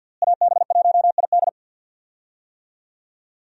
Morse code, heard as a keyed tone, about 52 wpm, 700 Hz. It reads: AB1ID
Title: USA Call Signs - 5 Characters - Repeated 3x - 50wpm